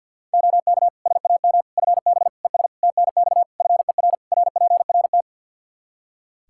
Morse code, heard as a keyed tone, about 48 wpm, 700 Hz. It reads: OK SAM FB ES TNX FER RPRT
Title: Practice QSO - Exchange 2 - Info - 50wpm